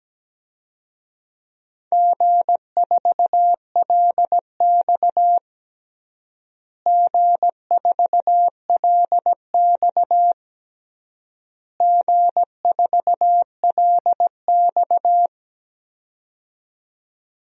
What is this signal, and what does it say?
Morse code, heard as a keyed tone, about 17 wpm, 700 Hz. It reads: G4LX G4LX G4LX
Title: DXCC-100 Call Signs - 4 Characters - Repeated 3x - 17wpm